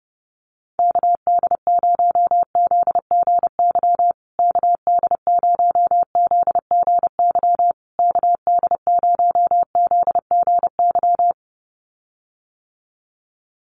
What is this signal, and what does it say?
Morse code, heard as a keyed tone, about 30 wpm, 700 Hz. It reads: KD0ZGY KD0ZGY KD0ZGY
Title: USA Call Signs Repeated 3x 30wpm